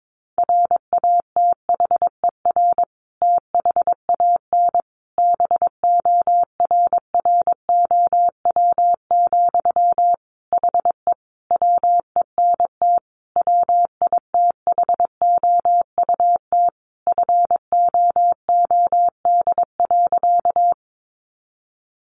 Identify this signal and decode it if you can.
Morse code, heard as a keyed tone, about 22 wpm, 700 Hz. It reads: RATHER THAN BORROW, HE WENT WITHOUT FOOD.
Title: Sentences from Top 1500 Words 22wpm